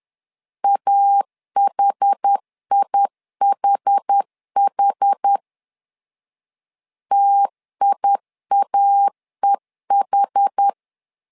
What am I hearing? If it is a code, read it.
Morse code, heard as a keyed tone, about 10 wpm, 800 Hz. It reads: AHIHH TIAEH
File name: LEZIONI / 2 UHDN / 2 UH 10wpm 1714